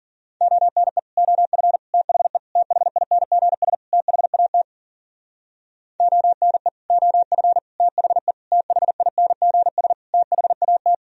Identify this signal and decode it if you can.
Morse code, heard as a keyed tone, about 53 wpm, 700 Hz. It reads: ONE OF THE THINGS THAT ONE OF THE THINGS THAT